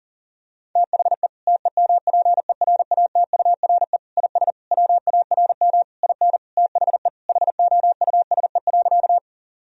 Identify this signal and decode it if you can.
Morse code, heard as a keyed tone, about 40 wpm, 700 Hz. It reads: THE TEMPERATURE IS WARM IN THE HOUSE.
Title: Sentences from Top 1000 Words - Review Part 4 - 40wpm